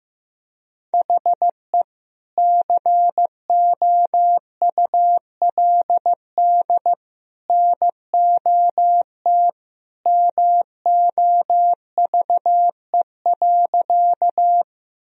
Morse code, heard as a keyed tone, about 15 wpm, 700 Hz. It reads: HE COULD NOT MOVE.
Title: Sentences from Top 1000 Words - Review Part 1 - 15wpm